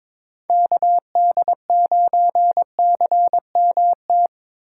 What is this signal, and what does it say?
Morse code, heard as a keyed tone, about 26 wpm, 700 Hz. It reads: KD9CMT
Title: Speed Racing - USA 2x3 Call Signs 22wpm